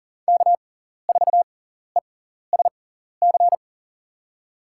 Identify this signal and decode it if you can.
Morse code, heard as a keyed tone, about 40 wpm, spaced as for 14 wpm, 700 Hz. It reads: K4ESC
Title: USA Call Signs - 5 Characters - 6x Character Spacing - 40wpm